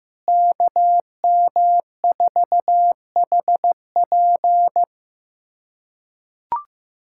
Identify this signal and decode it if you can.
Morse code, heard as a keyed tone, about 15 wpm, 700 Hz. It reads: KM4HP
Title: USA Callsign DE Callsign 15wpm